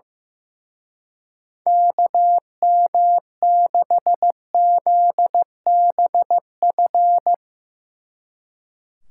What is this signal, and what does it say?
Morse code, heard as a keyed tone, about 16 wpm, 700 Hz. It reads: KM6ZBF
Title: Speed Racing - USA 2x3 Call Signs 15wpm